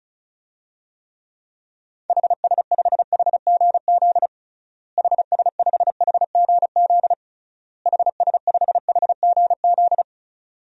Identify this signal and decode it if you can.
Morse code, heard as a keyed tone, about 36 wpm, 700 Hz. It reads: HS5HGZ HS5HGZ HS5HGZ
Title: DXCC-100 Call Signs - 6 Characters - Repeated 3x - 35wpm